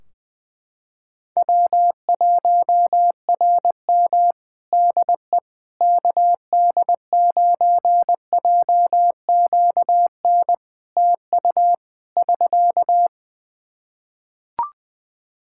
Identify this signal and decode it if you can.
Morse code, heard as a keyed tone, about 20 wpm, 700 Hz. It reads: W1RM DE KD9JQN TU <SK>
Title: K1USN SST QSO Practice 20wpm